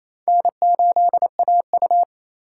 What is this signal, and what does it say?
Morse code, heard as a keyed tone, about 36 wpm, 700 Hz. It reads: N8AU